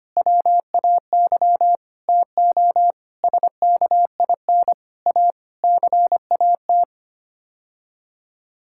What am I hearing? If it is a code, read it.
Morse code, heard as a keyed tone, about 25 wpm, 700 Hz. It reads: WAY TO SKIN A CAT
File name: Sets of 9 Words - 25wpm